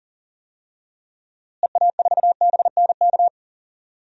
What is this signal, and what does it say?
Morse code, heard as a keyed tone, about 40 wpm, 700 Hz. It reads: EA4BNK
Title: DXCC-100 Call Signs - 6 Characters - Repeated 2x - 40wpm